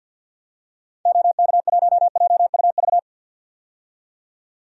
Morse code, heard as a keyed tone, about 53 wpm, 700 Hz. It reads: OK1JUV
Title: DXCC-100 Call Signs - 6 Characters - 50wpm